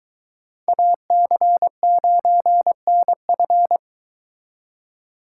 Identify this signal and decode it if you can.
Morse code, heard as a keyed tone, about 22 wpm, 700 Hz. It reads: AC9NF